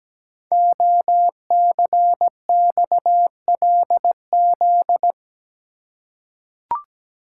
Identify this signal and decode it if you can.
Morse code, heard as a keyed tone, about 17 wpm, 700 Hz. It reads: OCXLZ
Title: Code Groups - 5 Letters - 17wpm